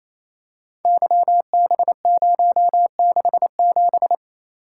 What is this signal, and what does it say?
Morse code, heard as a keyed tone, about 29 wpm, 700 Hz. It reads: YB067
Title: Code Groups - 5 Letter:Numbers - 28wpm